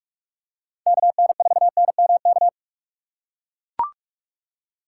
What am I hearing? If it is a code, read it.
Morse code, heard as a keyed tone, about 44 wpm, 700 Hz. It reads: KN4NMK